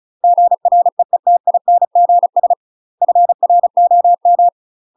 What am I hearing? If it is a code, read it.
Morse code, heard as a keyed tone, about 34 wpm, 700 Hz. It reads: GREETINGS FROM
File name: UK MASTER QSOs TRAINING/35WPM